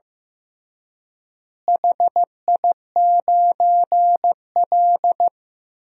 Morse code, heard as a keyed tone, about 15 wpm, 700 Hz. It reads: HI9L
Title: DXCC-100 Call Signs - 4 Characters - Repeated 2x - 15wpm